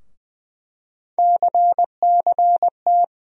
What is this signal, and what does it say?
Morse code, heard as a keyed tone, about 20 wpm, 700 Hz. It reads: CCT
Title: USA Call Sign Postfixes - 3 Letters 20wpm